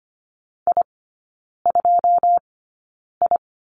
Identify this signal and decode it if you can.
Morse code, heard as a keyed tone, about 25 wpm, spaced as for 9 wpm, 700 Hz. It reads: I2I
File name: Lesson 17e - Letter u - Three Characters 25wpm